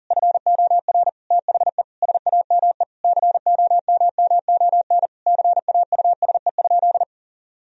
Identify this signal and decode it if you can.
Morse code, heard as a keyed tone, about 40 wpm, 700 Hz. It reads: FOR THE SAME COMMON CAUSE?
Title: Sentences from Top 1000 Words - Review Part 3 - 40wpm